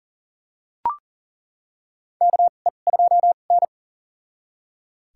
Morse code, heard as a keyed tone, about 40 wpm, spaced as for 29 wpm, 700 Hz. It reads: KE2N
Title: USA Call Signs - 4 Characters - 2x Character Spacing - 40wpm